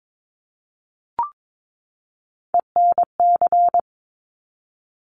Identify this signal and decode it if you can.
Morse code, heard as a keyed tone, about 22 wpm, 690 Hz. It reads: ENC
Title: Top Three Letter Combinations in Words 22wpm